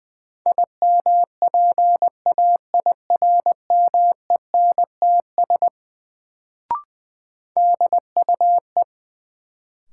Morse code, heard as a keyed tone, about 20 wpm, 700 Hz. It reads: IMPAIRMENTS DUE
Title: Sets of 5 Words Encore - New Word Drill - 20wpm